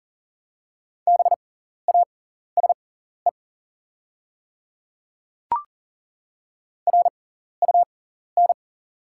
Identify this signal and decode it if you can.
Morse code, heard as a keyed tone, about 37 wpm, spaced as for 14 wpm, 700 Hz. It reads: BASE RUN
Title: 6x Character Spacing - Top 300 Words - 40wpm